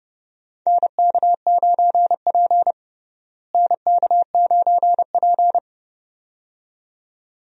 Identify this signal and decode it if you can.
Morse code, heard as a keyed tone, about 29 wpm, 700 Hz. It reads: NK9P NK9P
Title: USA Call Signs - 4 Characters - Repeated 2x - 30wpm